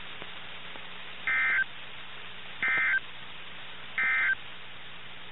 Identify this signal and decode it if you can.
Emergency Alert System (SAME) tones, heard as harsh buzzing bursts. It is heard from 1.2 to 4.4 s.
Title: NOAA Weather Radio - KIH61